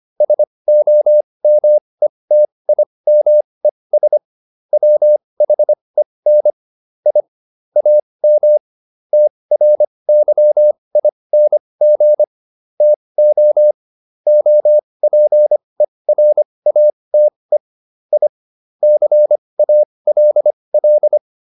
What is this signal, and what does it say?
Morse code, heard as a keyed tone, about 25 wpm, spaced as for 20 wpm, 600 Hz. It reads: SOMETIMES WHEN I AM TRYING TO OPERATE I CALL